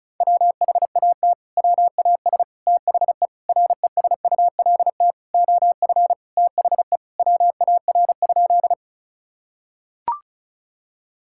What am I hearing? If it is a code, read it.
Morse code, heard as a keyed tone, about 34 wpm, 700 Hz. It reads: WHAT WAS THE RESULT OF THE WAR?
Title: Sentences from Top 1000 Words - Review Part 5 - 35wpm